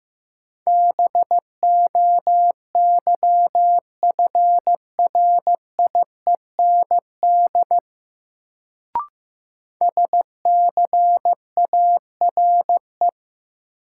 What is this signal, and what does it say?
Morse code, heard as a keyed tone, about 15 wpm, 700 Hz. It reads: BOYFRIEND SCARE
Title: Sentences from Top 1000 Words - Review Part 5 - New Word Drill - 15wpm